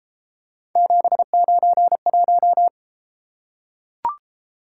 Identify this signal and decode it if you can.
Morse code, heard as a keyed tone, about 31 wpm, 700 Hz. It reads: 791